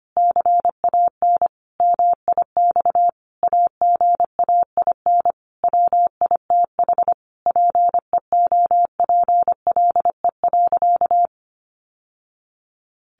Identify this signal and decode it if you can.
Morse code, heard as a keyed tone, about 25 wpm, 700 Hz. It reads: CAN MIX AGAIN WITH PEOPLE.
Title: Sentences from Top 1100 Words 25wpm